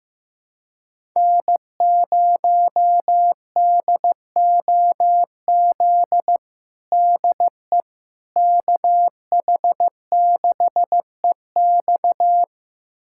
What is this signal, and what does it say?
Morse code, heard as a keyed tone, about 15 wpm, 700 Hz. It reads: N0DOZ DE KH6EX
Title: USA Callsign DE Callsign - 5 Characters - 15wpm